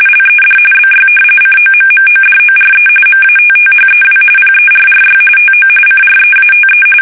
RTTY, about 45.45 baud. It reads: LED BY COMMODORE MATTHEW PERRY TO CREATE
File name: fsk850 / usnavy2